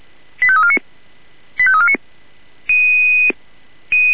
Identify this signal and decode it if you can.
Five-tone selective call (ZVEI), heard as quick runs of five tones, tones of about 70 ms. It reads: 85349 85349